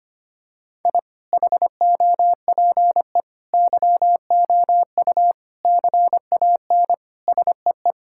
Morse code, heard as a keyed tone, about 25 wpm, 700 Hz. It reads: I HOPE YOU CAN SEE
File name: Sets of 9 Words - 25wpm